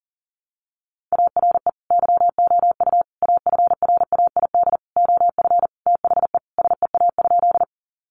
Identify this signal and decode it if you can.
Morse code, heard as a keyed tone, about 37 wpm, 700 Hz. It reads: ARE YOU AFRAID OF THE SEA?
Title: Sentences from Top 1500 Words 40wpm